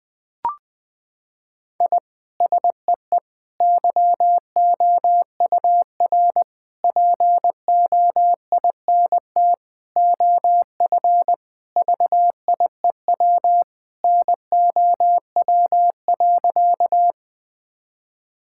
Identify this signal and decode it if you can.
Morse code, heard as a keyed tone, about 19 wpm, 700 Hz. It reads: I SEE YOUR POINT OF VIEW NOW.
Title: Sentences from Top 1000 Words - Review Part 6 - 20wpm